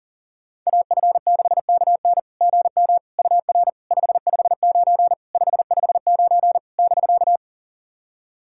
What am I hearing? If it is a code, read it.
Morse code, heard as a keyed tone, about 40 wpm, 700 Hz. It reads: AF6KN GM UR 559 559 <BK>
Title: SOTA QSO Practice 40wpm